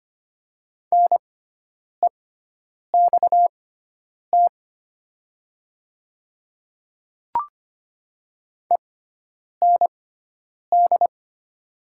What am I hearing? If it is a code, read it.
Morse code, heard as a keyed tone, about 25 wpm, spaced as for 9 wpm, 700 Hz. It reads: NEXT END
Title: 6x Character Spacing - Top 300 Words - 25wpm